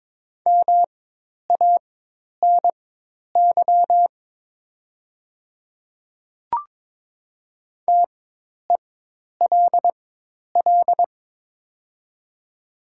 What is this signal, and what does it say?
Morse code, heard as a keyed tone, about 22 wpm, spaced as for 10 wpm, 700 Hz. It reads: MANY TELL
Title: 4x Character Spacing - Top 200 Words - 22wpm